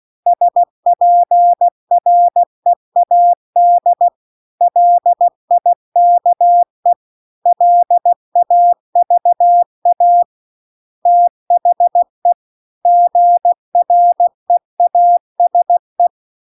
Morse code, heard as a keyed tone, about 16 wpm, 700 Hz. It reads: SPREAD LIKE LAVA THE GREASE
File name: CW-The-Big-Twoheaded-River-2-16WPM